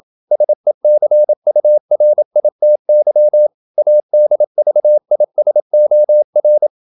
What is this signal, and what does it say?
Morse code, heard as a keyed tone, about 27 wpm, 600 Hz. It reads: SECURITY ADVISOR
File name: Never 0027wpm